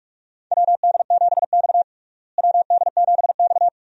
Morse code, heard as a keyed tone, about 44 wpm, 700 Hz. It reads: WD7X WD7X
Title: USA Call Signs - 4 Characters - Repeated 2x - 45wpm